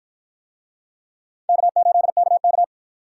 Morse code, heard as a keyed tone, about 53 wpm, 700 Hz. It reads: K8KK